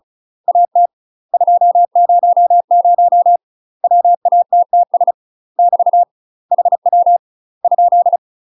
Morse code, heard as a keyed tone, about 36 wpm, 700 Hz. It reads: AT 200 WATTS <BT> HW ?
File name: UK MASTER QSOs TRAINING / 35WPM / UK MASTER QSOs 0042